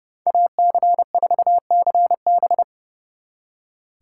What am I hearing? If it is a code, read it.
Morse code, heard as a keyed tone, about 40 wpm, 700 Hz. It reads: AC4CB